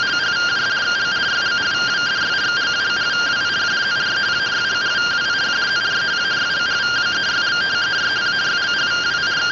Two-tone teleprinter signal, about 170 Hz shift.